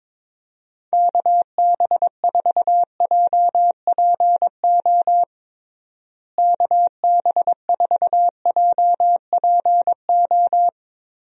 Morse code, heard as a keyed tone, about 22 wpm, 700 Hz. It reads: KB4JPO KB4JPO
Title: USA Call Signs - 6 Characters - Repeated 3x - 22wpm